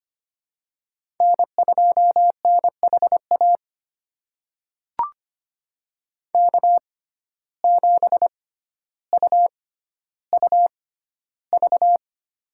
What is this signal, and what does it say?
Morse code, heard as a keyed tone, about 25 wpm, spaced as for 13 wpm, 700 Hz. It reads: N2NHA K7UUV
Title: USA Call Signs - 5 Characters - 6x Character Spacing - 25wpm